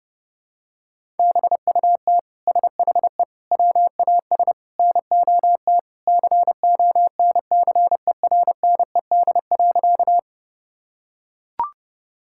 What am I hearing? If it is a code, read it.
Morse code, heard as a keyed tone, about 30 wpm, 700 Hz. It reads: BUT SHE WAS NOT CONCERNED.
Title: Sentences from Top 1000 Words - Review Part 6 - 30wpm